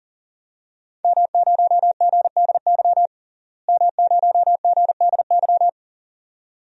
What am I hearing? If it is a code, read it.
Morse code, heard as a keyed tone, about 40 wpm, 700 Hz. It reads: M0GDY M0GDY
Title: DXCC-100 Call Signs - 5 Characters - Repeated 2x - 40wpm